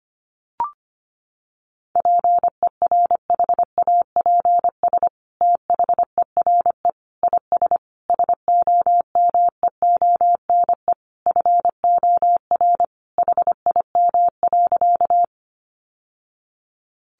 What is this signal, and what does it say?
Morse code, heard as a keyed tone, about 25 wpm, 700 Hz. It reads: PERHAPS THERE IS SOMEONE FOR HIM.
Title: Sentences from Top 600 Words 25wpm